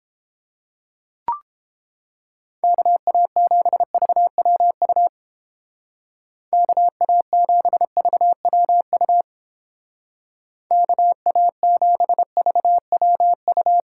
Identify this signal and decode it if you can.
Morse code, heard as a keyed tone, about 29 wpm, 700 Hz. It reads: KA7VWU KA7VWU KA7VWU